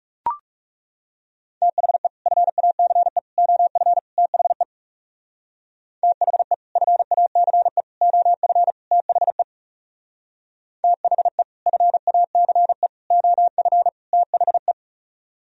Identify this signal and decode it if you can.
Morse code, heard as a keyed tone, about 36 wpm, 700 Hz. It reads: THE FACE OF THE THE FACE OF THE THE FACE OF THE
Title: Speed-Racing - Sets of 4 Words 30wpm